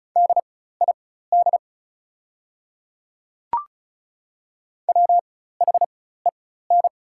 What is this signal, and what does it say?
Morse code, heard as a keyed tone, about 34 wpm, spaced as for 16 wpm, 700 Hz. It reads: DID WHEN
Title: 4x Character Spacing - Top 100 Words - 35wpm